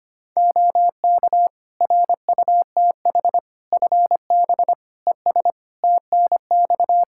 Morse code, heard as a keyed tone, about 25 wpm, 700 Hz. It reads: OK RUTH FB ES TNX
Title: Practice QSO - Exchange 2 - Info - 25wpm